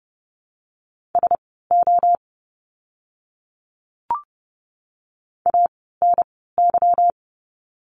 Morse code, heard as a keyed tone, about 30 wpm, spaced as for 17 wpm, 700 Hz. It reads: SO ANY
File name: course / 30:10wpm / 230 Lesson 37f - Letter x - Words 30wpm